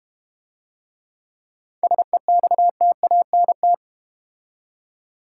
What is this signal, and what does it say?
Morse code, heard as a keyed tone, about 30 wpm, 690 Hz. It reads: SEXTANT
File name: practice / 25wpm / Difficult Words - Speed-Racing with Hints - 25wpm